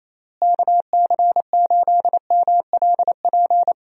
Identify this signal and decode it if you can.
Morse code, heard as a keyed tone, about 27 wpm, 700 Hz. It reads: KC8MLP
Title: USA Call Signs - 6 Characters - Repeated 3x - 28wpm